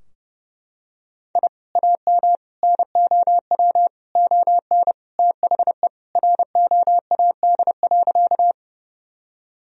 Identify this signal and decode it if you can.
Morse code, heard as a keyed tone, about 30 wpm, 700 Hz. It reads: I AM NOW ON THE ROAD.
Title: Sentences from Top 1000 Words - Review Part 2 - 30wpm